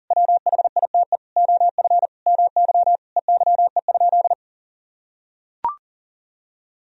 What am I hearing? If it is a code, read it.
Morse code, heard as a keyed tone, about 40 wpm, 700 Hz. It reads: WHITE OF MY EYE?
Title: Sentences from Top 500 Words and Abbreviations 40wpm